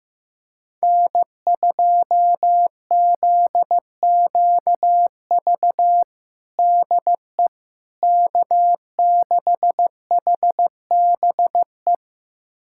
Morse code, heard as a keyed tone, about 15 wpm, 700 Hz. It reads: N2ZQV DE K6HBE
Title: USA Callsign DE Callsign 15wpm